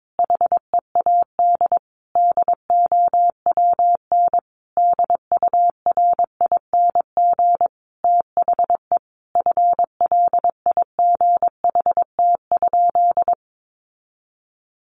Morse code, heard as a keyed tone, about 22 wpm, 700 Hz. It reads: HEAD DOWN DURING THE FLIGHT?
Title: Sentences from Top 1000 Words 22wpm